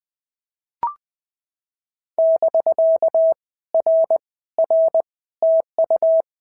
Morse code, heard as a keyed tone, about 20 wpm, 650 Hz. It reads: <BK> R R TU
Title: POTA QSO Practice - 20wpm